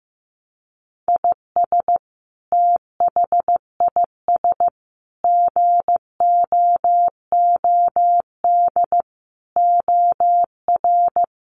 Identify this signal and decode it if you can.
Morse code, heard as a keyed tone, about 15 wpm, 700 Hz. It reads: IS THIS GOOD OR
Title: Sentences from Top 700 Words 15wpm